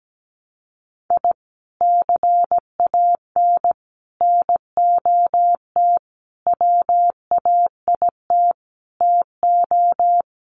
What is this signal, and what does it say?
Morse code, heard as a keyed tone, about 17 wpm, 700 Hz. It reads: I CAN NOT WAIT TO
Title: Sets of 5 Words Encore 17wpm